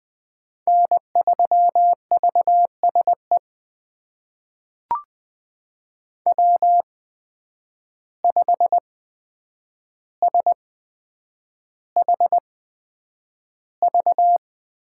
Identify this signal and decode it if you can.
Morse code, heard as a keyed tone, about 20 wpm, spaced as for 9 wpm, 700 Hz. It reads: N3VSE W5SHV
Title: USA Call Signs - 5 Characters - 8x Character Spacing - 20wpm